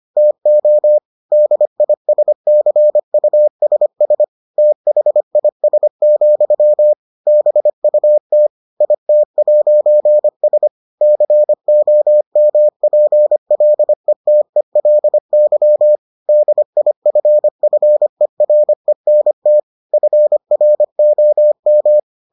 Morse code, heard as a keyed tone, about 25 wpm, 600 Hz. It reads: TO DISCUSS THIS, BUT IT'S COMPLETELY DIFFERENT FROM